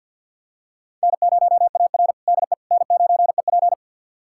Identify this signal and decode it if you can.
Morse code, heard as a keyed tone, about 45 wpm, 700 Hz. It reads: N0AR DE N9EP
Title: USA Callsign DE Callsign - 4 Characters - 50wpm